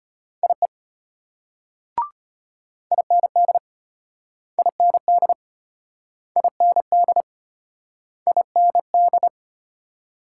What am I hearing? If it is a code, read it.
Morse code, heard as a keyed tone, about 33 wpm, 700 Hz. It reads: IE IND IND IND IND